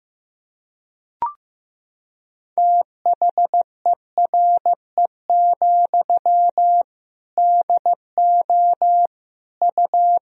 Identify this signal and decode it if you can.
Morse code, heard as a keyed tone, about 15 wpm, 700 Hz. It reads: THERE, DO U
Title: Sentences from Top 300 Words and Abbreviations 15wpm